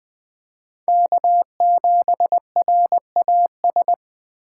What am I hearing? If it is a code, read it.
Morse code, heard as a keyed tone, about 21 wpm, 700 Hz. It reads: K7RAS
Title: USA Call Signs - 3x Character Spacing - 20wpm